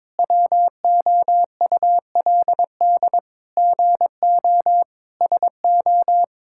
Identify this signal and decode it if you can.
Morse code, heard as a keyed tone, about 22 wpm, 700 Hz. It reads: WOULD GO SO
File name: practice / 22wpm / Sets of 8 Words - 22wpm